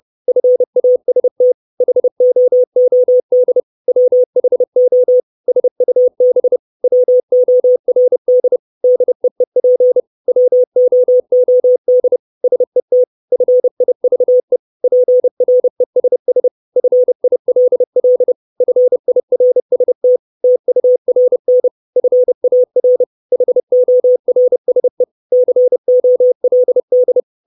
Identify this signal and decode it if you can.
Morse code, heard as a keyed tone, about 30 wpm, 500 Hz. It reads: FAST HOOD WHO SUB WORD DEEP WOOD SET FIVE PRESS FILL FIRST TURN FAR HORSE COLD
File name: 632 Words 30WPM